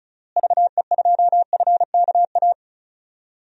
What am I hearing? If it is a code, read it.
Morse code, heard as a keyed tone, about 36 wpm, 700 Hz. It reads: VE2FKA
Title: DXCC-100 Call Signs - 6 Characters - Repeated 3x - 35wpm